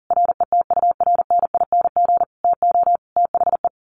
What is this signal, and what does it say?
Morse code, heard as a keyed tone, about 40 wpm, 700 Hz. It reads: RETURNING TO THE
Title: A Sound Of Thunder 40wpm